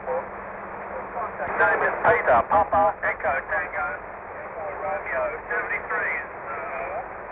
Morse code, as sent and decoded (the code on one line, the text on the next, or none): none